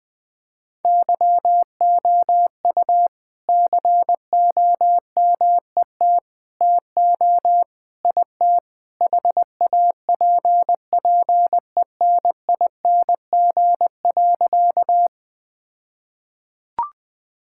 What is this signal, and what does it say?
Morse code, heard as a keyed tone, about 20 wpm, 700 Hz. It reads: YOU COMET TO IT HAPPENING.
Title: All-Age Appropriate Jokes - 20wpm